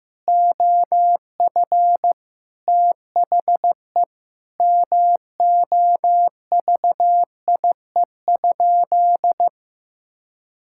Morse code, heard as a keyed tone, about 16 wpm, 700 Hz. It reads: OF THE MOVIE?
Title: Sentences from Top 1000 Words - Review Part 5 - 15wpm